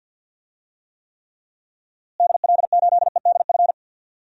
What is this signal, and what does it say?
Morse code, heard as a keyed tone, about 47 wpm, 700 Hz. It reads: DL8EDF
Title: DXCC-100 Call Signs - 6 Characters - Repeated 3x - 50wpm